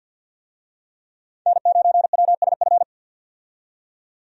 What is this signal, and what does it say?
Morse code, heard as a keyed tone, about 52 wpm, 700 Hz. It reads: N9WSF
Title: USA Call Signs - 5 Characters - 50wpm